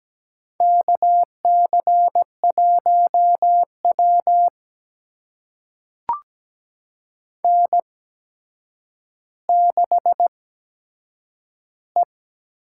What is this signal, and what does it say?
Morse code, heard as a keyed tone, about 17 wpm, spaced as for 8 wpm, 700 Hz. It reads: KC1W N6E
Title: USA Call Signs - 4 Characters - 8x Character Spacing - 17wpm